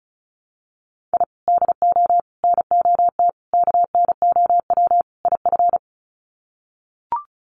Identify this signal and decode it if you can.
Morse code, heard as a keyed tone, about 34 wpm, 700 Hz. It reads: I DO NOT KNOW IF
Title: Sets of 5 Words 35wpm